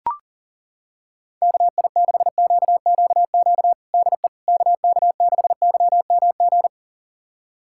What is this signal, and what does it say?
Morse code, heard as a keyed tone, about 40 wpm, 700 Hz. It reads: KI6QQQ DE KK6YMG
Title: USA Callsign DE Callsign 40wpm